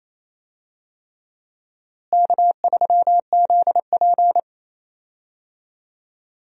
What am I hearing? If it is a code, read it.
Morse code, heard as a keyed tone, about 28 wpm, 700 Hz. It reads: K3ZP
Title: USA Call Signs - 4 Characters - Repeated 3x - 28wpm